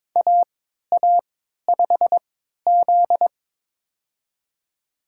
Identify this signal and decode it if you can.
Morse code, heard as a keyed tone, about 22 wpm, spaced as for 12 wpm, 700 Hz. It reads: AA5Z
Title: USA Call Signs - 4 Characters - 3x Character Spacing - 22wpm